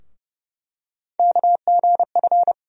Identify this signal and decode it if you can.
Morse code, heard as a keyed tone, about 30 wpm, 700 Hz. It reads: KGF
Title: USA Call Sign Postfixes - 3 Letters 30wpm